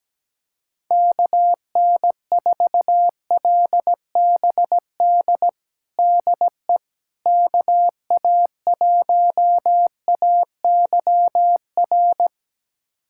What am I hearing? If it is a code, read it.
Morse code, heard as a keyed tone, about 17 wpm, 700 Hz. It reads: KN4LBD DE KA1AYR
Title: USA Callsign DE Callsign - 6 Characters - 17wpm